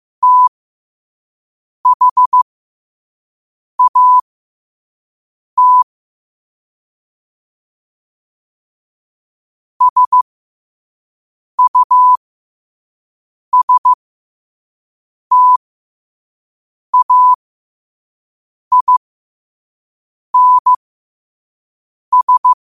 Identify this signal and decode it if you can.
Morse code, heard as a keyed tone, about 15 wpm, spaced as for 5 wpm, 1000 Hz. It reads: THAT SUSTAINS
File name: qotdfiles / 5wpm